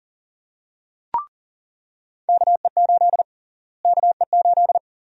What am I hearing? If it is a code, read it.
Morse code, heard as a keyed tone, about 40 wpm, 700 Hz. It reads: KE8 KE8
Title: USA Call Sign Prefixes - Repeated 2x - 40wpm